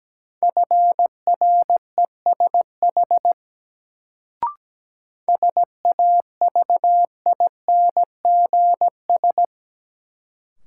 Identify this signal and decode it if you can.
Morse code, heard as a keyed tone, about 17 wpm, 700 Hz. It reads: FRESH SAVINGS
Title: Sentences from Top 1200 Words - New Word Drill - 17wpm